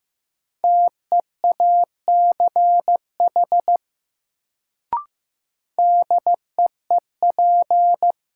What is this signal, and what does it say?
Morse code, heard as a keyed tone, about 16 wpm, 700 Hz. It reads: TEACH DEEP
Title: Sentences from Top 500 Words - New Word Drill - 15wpm